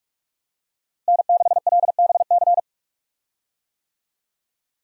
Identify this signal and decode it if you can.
Morse code, heard as a keyed tone, about 42 wpm, 700 Hz. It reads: N6LBC
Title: USA Call Signs - 5 Characters - Repeated 3x - 45wpm